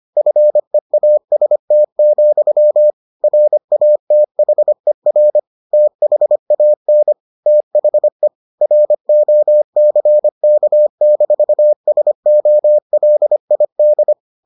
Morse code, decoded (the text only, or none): FEAST, RATHER THAN THE ROCK-SOLID